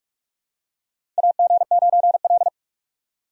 Morse code, heard as a keyed tone, about 46 wpm, 700 Hz. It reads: AG9L
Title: USA Call Signs - 4 Characters - 45wpm